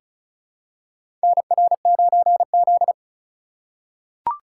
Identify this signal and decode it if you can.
Morse code, heard as a keyed tone, about 34 wpm, 700 Hz. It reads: NR9Z
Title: USA Call Signs - 4 Characters - 2x Character Spacing - 35wpm